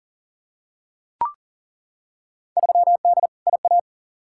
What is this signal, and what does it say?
Morse code, heard as a keyed tone, about 40 wpm, 700 Hz. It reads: 3D IA
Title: Field Day Exchanges Encore - 40wpm